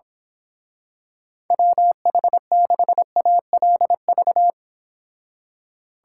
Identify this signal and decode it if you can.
Morse code, heard as a keyed tone, about 25 wpm, 700 Hz. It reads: WH6ALV
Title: Speed Racing - USA 2x3 Call Signs 22wpm